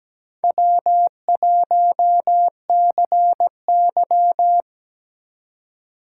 Morse code, heard as a keyed tone, about 17 wpm, 700 Hz. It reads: W1CY
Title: USA Call Signs - 4 Characters - 4x Character Spacing - 17wpm